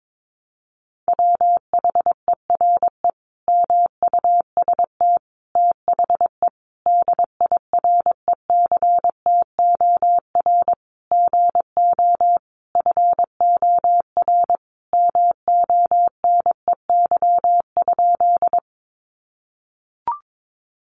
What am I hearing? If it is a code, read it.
Morse code, heard as a keyed tone, about 22 wpm, 700 Hz. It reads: WHERE MUST THE DIRECTOR GO FOR MONEY?
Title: Sentences from Top 1300 Words 22wpm